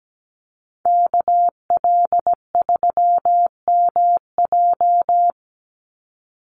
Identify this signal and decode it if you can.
Morse code, heard as a keyed tone, about 17 wpm, 700 Hz. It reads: KL3MJ
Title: USA Call Signs 17wpm